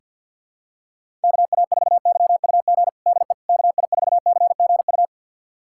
Morse code, heard as a keyed tone, about 47 wpm, 700 Hz. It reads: KA4YUG DE KI4CGU
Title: USA Callsign DE Callsign - 6 Characters - 50wpm